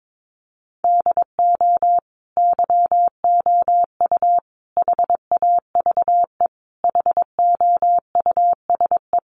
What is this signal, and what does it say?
Morse code, heard as a keyed tone, about 22 wpm, 700 Hz. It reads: DO YOU HAVE HOUSE
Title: Sentences from Top 1100 Words 22wpm